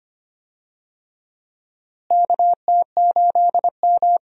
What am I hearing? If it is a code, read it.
Morse code, heard as a keyed tone, about 25 wpm, 700 Hz. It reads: KT8M
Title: USA Call Signs - 4 Characters - Repeated 3x - 25wpm